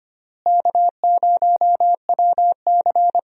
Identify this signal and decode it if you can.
Morse code, heard as a keyed tone, about 25 wpm, 700 Hz. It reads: K0WC
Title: USA Call Signs - 4 Characters - Repeated 3x - 25wpm